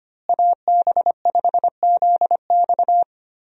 Morse code, heard as a keyed tone, about 25 wpm, 700 Hz. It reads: AB5ZX
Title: USA Call Signs - 5 Characters - Repeated 3x - 25wpm